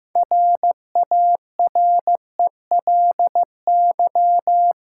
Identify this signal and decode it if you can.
Morse code, heard as a keyed tone, about 15 wpm, 700 Hz. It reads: RARELY
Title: Six Letter Words - 15wpm